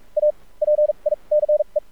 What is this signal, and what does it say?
Morse code, heard as a keyed tone, about 46 wpm, spaced as for 29 wpm, 600 Hz. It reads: APICE
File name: archived / CRSnet